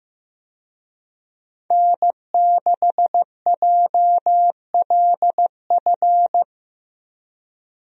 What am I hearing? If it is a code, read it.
Morse code, heard as a keyed tone, about 15 wpm, 700 Hz. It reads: N6JLF
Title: USA Call Signs - 5 Characters - 15wpm